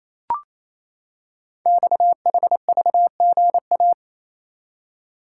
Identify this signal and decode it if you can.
Morse code, heard as a keyed tone, about 28 wpm, 700 Hz. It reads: XHVGA
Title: Code Groups - 5 Letters - 28wpm